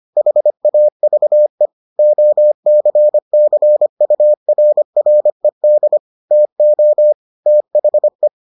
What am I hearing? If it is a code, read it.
Morse code, heard as a keyed tone, about 25 wpm, 600 Hz. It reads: HAVE OCCURRED TO THE